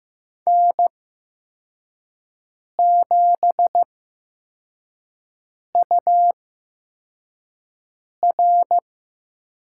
Morse code, heard as a keyed tone, about 15 wpm, spaced as for 4 wpm, 700 Hz. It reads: N7UR
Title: USA Call Signs - 5 Characters - 8x Character Spacing - 15wpm